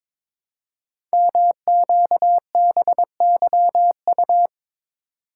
Morse code, heard as a keyed tone, about 22 wpm, 700 Hz. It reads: MQBYU
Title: Code Groups - 5 Letters - 22wpm